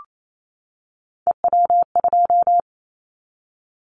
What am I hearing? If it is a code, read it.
Morse code, heard as a keyed tone, about 28 wpm, 700 Hz. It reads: EW2
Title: Callsign Prefixes for Top 100 DXCC - 28wpm